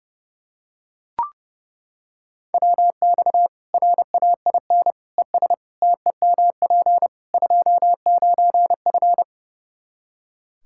Morse code, heard as a keyed tone, about 32 wpm, 700 Hz. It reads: WX RAIN ES TEMP 29F
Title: Practice QSO - Exchange 2 - Info - 30wpm